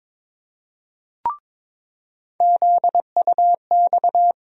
Morse code, heard as a keyed tone, about 22 wpm, 700 Hz. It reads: ZUX
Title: USA Call Sign Postfixes - 3 Letters 22wpm